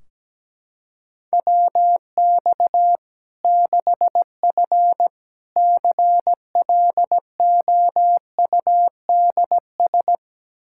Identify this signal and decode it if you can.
Morse code, heard as a keyed tone, about 17 wpm, 700 Hz. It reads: WX 6F CLOUDS
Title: Practice QSO - Exchange 2 - Info - 17wpm